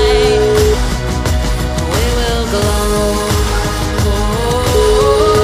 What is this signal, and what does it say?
Ringback tone (Europe, 425 Hz), with noise.